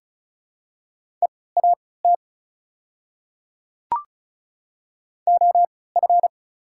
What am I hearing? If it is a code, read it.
Morse code, heard as a keyed tone, about 34 wpm, spaced as for 20 wpm, 700 Hz. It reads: EAT OF
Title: 3x Character Spacing - Top 500 Words - 35wpm